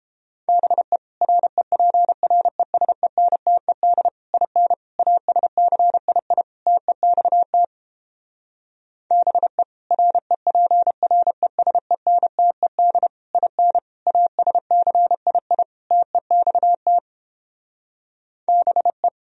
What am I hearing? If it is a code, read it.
Morse code, heard as a keyed tone, about 29 wpm, 700 Hz. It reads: BE REPRESENTED IN ASCII TEXT BE REPRESENTED IN ASCII TEXT BE